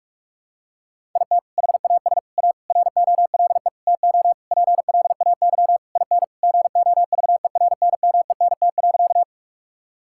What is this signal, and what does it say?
Morse code, heard as a keyed tone, about 46 wpm, 700 Hz. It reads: IT HAS A ROLE TO PLAY IN GOVERNMENT.
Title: Sentences from Top 1000 Words - Review Part 6 - 45wpm